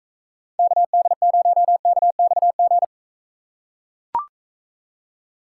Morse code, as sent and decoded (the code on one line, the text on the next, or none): -.- -.. ----- -.- -..- --.
KD0KXG